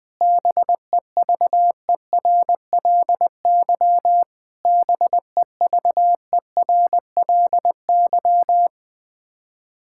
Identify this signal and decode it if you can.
Morse code, heard as a keyed tone, about 20 wpm, 700 Hz. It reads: BEVERLY BEVERLY
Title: Practice QSO - Exchange 1 - Beginning - 20wpm